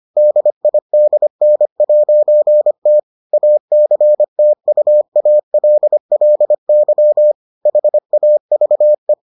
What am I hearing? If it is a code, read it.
Morse code, heard as a keyed tone, about 25 wpm, 600 Hz. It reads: DIDN'T ACTUALLY HAVE